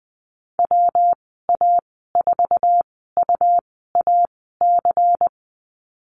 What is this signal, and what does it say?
Morse code, heard as a keyed tone, about 20 wpm, spaced as for 14 wpm, 700 Hz. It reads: WA4UAC
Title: 122 Lesson 20g - Number 3 - Call Signs 20wpm